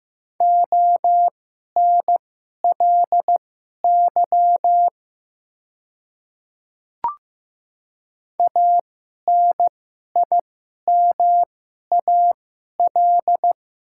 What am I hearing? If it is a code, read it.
Morse code, heard as a keyed tone, about 15 wpm, spaced as for 11 wpm, 700 Hz. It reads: ONLY ANIMAL
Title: 2x Character Spacing - Top 200 Words - 15wpm